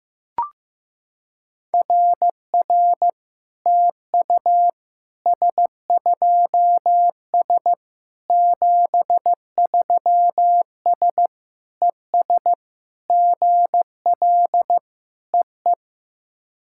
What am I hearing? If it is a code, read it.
Morse code, heard as a keyed tone, about 15 wpm, 700 Hz. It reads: RR TU S2S 73S ES GL EE
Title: SOTA QSO Practice 15wpm